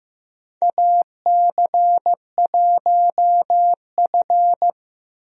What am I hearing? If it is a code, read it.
Morse code, heard as a keyed tone, about 15 wpm, 700 Hz. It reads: AC1F
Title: USA Call Signs - 4 Characters - Repeated 2x - 15wpm